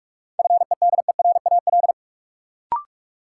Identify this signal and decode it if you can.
Morse code, heard as a keyed tone, about 46 wpm, 700 Hz. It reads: FEDERAL
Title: Sets of 3 Words - New Word Drill - 45wpm